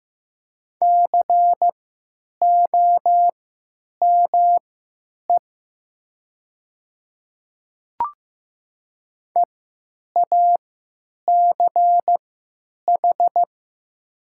Morse code, heard as a keyed tone, about 15 wpm, spaced as for 9 wpm, 700 Hz. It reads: COME EACH
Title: 3x Character Spacing - Top 200 Words - 15wpm